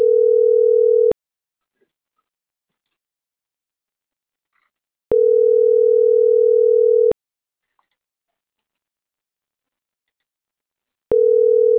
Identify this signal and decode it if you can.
Ringback tone (North America), 440 + 480 Hz.